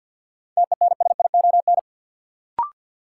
Morse code, heard as a keyed tone, about 53 wpm, 700 Hz. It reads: TENSION
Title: Sentences from Top 1600 Words - New Word Drill Alt - 50wpm